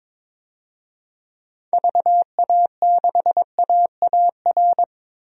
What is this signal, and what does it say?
Morse code, heard as a keyed tone, about 22 wpm, 700 Hz. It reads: VA6AAR
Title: DXCC-100 Call Signs - 6 Characters - 22wpm